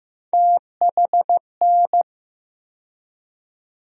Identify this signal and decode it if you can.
Morse code, heard as a keyed tone, about 15 wpm, 700 Hz. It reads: THN
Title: USA Call Sign Postfixes - 3 Letters 15wpm